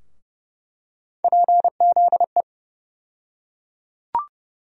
Morse code, heard as a keyed tone, about 30 wpm, 700 Hz. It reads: PZE
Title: USA Call Sign Postfixes - 3 Letters 30wpm